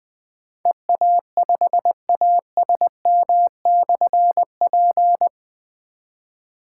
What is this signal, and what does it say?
Morse code, heard as a keyed tone, about 20 wpm, 700 Hz. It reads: EA5ASM/P
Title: Difficult Call Signs 20wpm